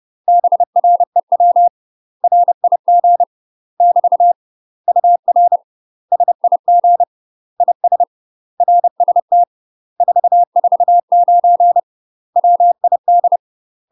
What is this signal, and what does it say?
Morse code, heard as a keyed tone, about 30 wpm, 700 Hz. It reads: BREW RIG <BT> UR SIG IS RST 449 WID
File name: UK MASTER QSOs TRAINING / 30WPM / UK MASTER QSOs 0016